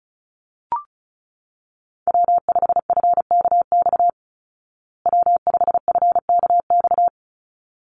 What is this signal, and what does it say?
Morse code, heard as a keyed tone, about 34 wpm, 700 Hz. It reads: W5FKX W5FKX
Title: Call Signs Repeated 3x - 35wpm